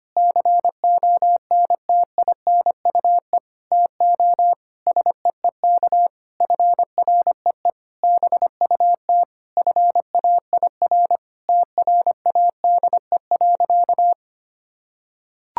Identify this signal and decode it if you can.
Morse code, heard as a keyed tone, about 25 wpm, 700 Hz. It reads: CONTINUE TO SEEK FREE BUT FAIR TRADE.
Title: Sentences from Top 1600 Words 25wpm